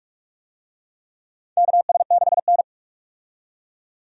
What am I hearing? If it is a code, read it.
Morse code, heard as a keyed tone, about 46 wpm, 700 Hz. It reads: KS6N
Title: USA Call Signs - 4 Characters - 45wpm